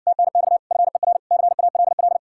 Morse code, heard as a keyed tone, about 53 wpm, 700 Hz. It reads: TNX FER CALL